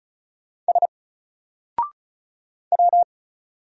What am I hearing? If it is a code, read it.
Morse code, heard as a keyed tone, about 35 wpm, 700 Hz. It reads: S W
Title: Single Character (Letter:Number:Punctuation:Prosign) 35wpm